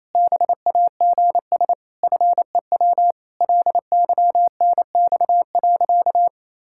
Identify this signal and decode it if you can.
Morse code, heard as a keyed tone, about 28 wpm, 700 Hz. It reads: BAGS FEW LYNX.
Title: Pangram Challenge v2 - 28wpm